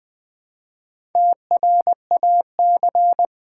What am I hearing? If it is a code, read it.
Morse code, heard as a keyed tone, about 20 wpm, 700 Hz. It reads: TRAC
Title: Build A Word - 1 Letter At A Time - 3 to 5-Letter Words - 20wpm